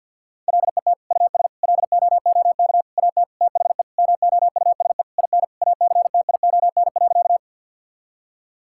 Morse code, heard as a keyed tone, about 50 wpm, 700 Hz. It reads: LET US LOOK AT THE MOUSE IN ACTION.
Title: Sentences from Top 1000 Words - Review Part 5 - 50wpm